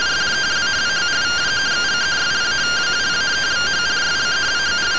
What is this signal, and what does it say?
Two-tone teleprinter signal, about 170 Hz shift.